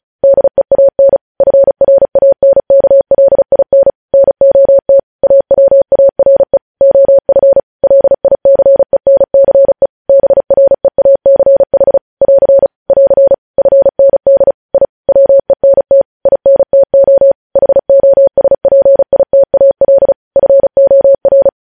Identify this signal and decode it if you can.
Morse code, heard as a keyed tone, about 36 wpm, 550 Hz. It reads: DEAN FRANKLIN NOT AWARE OF LICENCE BREACH <AR> <AR> FND I WENT INTO HOSPITAL FOR
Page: news headlines in morse code at 35 wpm